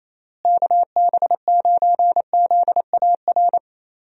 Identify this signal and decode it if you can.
Morse code, heard as a keyed tone, about 28 wpm, 700 Hz. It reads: KB9ZAR
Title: USA Call Signs - 6 Characters - Repeated 3x - 28wpm